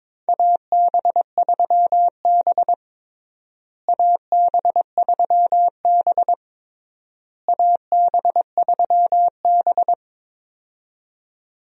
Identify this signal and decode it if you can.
Morse code, heard as a keyed tone, about 22 wpm, 700 Hz. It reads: AB3B AB3B AB3B
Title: USA Call Signs - 4 Characters - Repeated 3x - 22wpm